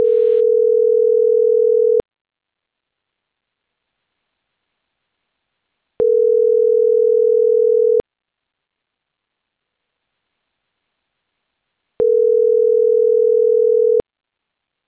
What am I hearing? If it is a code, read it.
Ringback tone (North America), 440 + 480 Hz.
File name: CENTRAL CAEA / out agcaea02